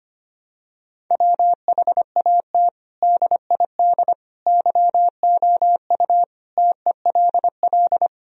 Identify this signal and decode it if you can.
Morse code, heard as a keyed tone, about 25 wpm, 700 Hz. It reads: WHAT DID YOU TELL
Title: Sentences from Top 1000 Words - Review Part 4 - 25wpm